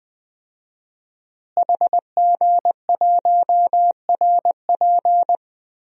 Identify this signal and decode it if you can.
Morse code, heard as a keyed tone, about 21 wpm, 700 Hz. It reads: HG1RP
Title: DXCC-100 Call Signs - 5 Characters - Repeated 3x - 20wpm